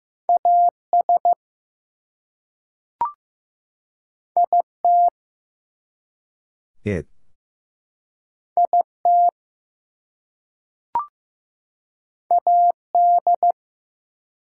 Morse code, heard as a keyed tone, about 15 wpm, 700 Hz. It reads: AS IT IT AD
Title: Two Letter Words - 15wpm